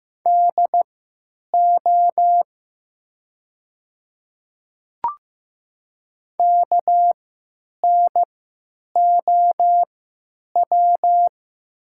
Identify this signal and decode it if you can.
Morse code, heard as a keyed tone, about 15 wpm, spaced as for 8 wpm, 700 Hz. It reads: DO KNOW
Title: 3x Character Spacing - Top 100 Words - 15wpm